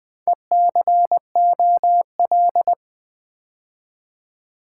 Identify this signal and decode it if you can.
Morse code, heard as a keyed tone, about 20 wpm, 700 Hz. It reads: ECOL
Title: Build A Word - 4 Letters At A Time - 10-Letter Words - 20wpm